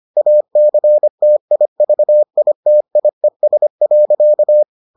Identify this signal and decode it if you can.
Morse code, heard as a keyed tone, about 25 wpm, 600 Hz. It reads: ACTIVITIES.